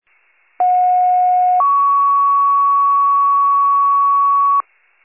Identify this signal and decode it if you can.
Two-tone page, heard as a one-second tone, then a three-second tone, about 730 then 1120 Hz. It is heard from 0.6 to 4.6 s.
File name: pager / 21 RESCUE / wcalert